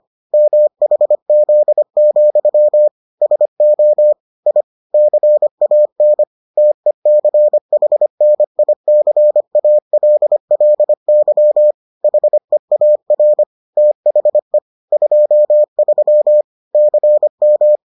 Morse code, heard as a keyed tone, about 25 wpm, 600 Hz. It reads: MHZ, SO I CAN TECHNICALLY HEAR THE 23 CM